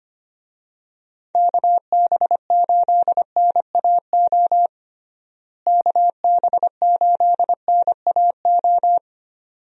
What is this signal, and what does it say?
Morse code, heard as a keyed tone, about 25 wpm, 700 Hz. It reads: KB8NAO KB8NAO
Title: USA Call Signs - 6 Characters - Repeated 2x - 25wpm